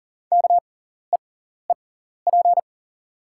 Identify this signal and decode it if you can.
Morse code, heard as a keyed tone, about 40 wpm, spaced as for 14 wpm, 690 Hz. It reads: KEEP
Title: 6x Character Spacing - Top 300 Words - 40wpm